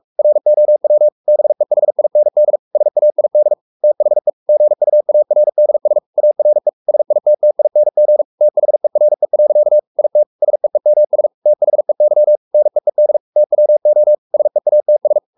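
Morse code, heard as a keyed tone, about 44 wpm, 600 Hz. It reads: ROW BEHIND SAID THE GUARDS ARE SITTING THERE. IT SEEMS THEY NEED TWO SEATS
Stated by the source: Morse code at 43 wpm sending Ken Follett's novel Never in English, from CW med Gnister.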